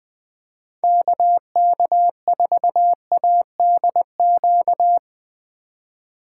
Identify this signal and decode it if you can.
Morse code, heard as a keyed tone, about 20 wpm, 700 Hz. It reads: KK4ADQ
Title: 2023 POTA Hunter Call Signs - 20wpm